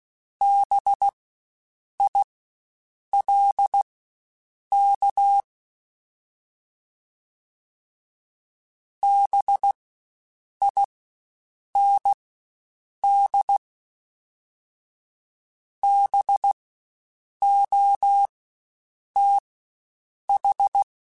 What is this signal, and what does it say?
Morse code, heard as a keyed tone, about 16 wpm, spaced as for 7 wpm, 800 Hz. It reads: BILK BIND BOTH
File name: Spaced Alphas / Spaced Alphas mp3s / Bs sp @15